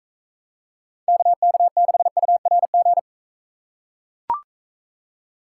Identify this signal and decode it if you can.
Morse code, heard as a keyed tone, about 40 wpm, 700 Hz. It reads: KK6URG